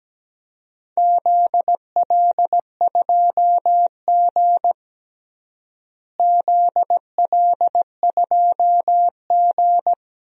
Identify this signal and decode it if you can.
Morse code, heard as a keyed tone, about 17 wpm, 700 Hz. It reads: ZL2G ZL2G
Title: DXCC-100 Call Signs - 4 Characters - Repeated 2x - 17wpm